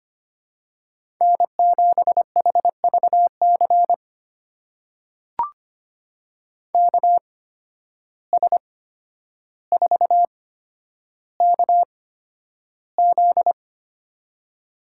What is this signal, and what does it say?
Morse code, heard as a keyed tone, about 25 wpm, spaced as for 11 wpm, 700 Hz. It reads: N7HVC KS4KZ
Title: USA Call Signs - 5 Characters - 8x Character Spacing - 25wpm